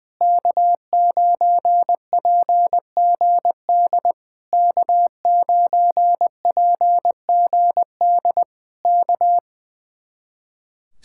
Morse code, heard as a keyed tone, about 20 wpm, 700 Hz. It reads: K9PGD K9PGD K
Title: Speed-Racing Common QSO Phrases - 20wpm